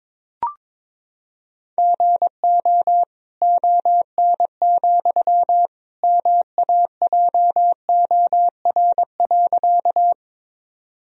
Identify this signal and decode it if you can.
Morse code, heard as a keyed tone, about 22 wpm, 700 Hz. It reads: GO ON, MAJOR.
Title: Sentences from Top 1000 Words - Review Part 4 - 22wpm